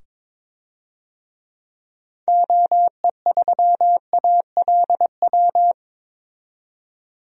Morse code, heard as a keyed tone, about 21 wpm, 700 Hz. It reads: OE3ALW
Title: DXCC-100 Call Signs - 6 Characters - 22wpm